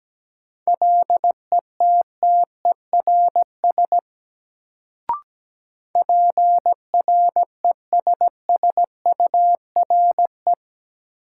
Morse code, heard as a keyed tone, about 17 wpm, 700 Hz. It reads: LETTERS PRESSURE
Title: Sentences from Top 1000 Words - Review Part 6 - New Word Drill - 17wpm